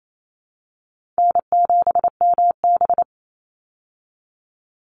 Morse code, heard as a keyed tone, about 28 wpm, 700 Hz. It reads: N7MB
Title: Easy Call Signs Repeated 3x - 28wpm